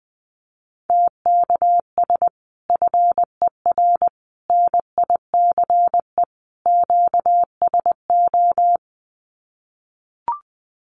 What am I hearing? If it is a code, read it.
Morse code, heard as a keyed tone, about 19 wpm, 700 Hz. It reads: TKS FER NICE QSO
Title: Common QSO Phrases 20wpm